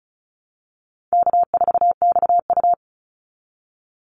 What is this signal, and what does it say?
Morse code, heard as a keyed tone, about 34 wpm, 700 Hz. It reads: K4XU
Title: Easy Call Signs Repeated 2x - 35wpm